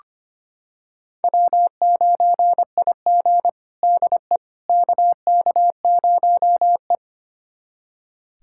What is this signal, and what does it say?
Morse code, heard as a keyed tone, about 25 wpm, 700 Hz. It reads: W9IG DE KK0E
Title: USA Callsign DE Callsign - 4 Characters - 25wpm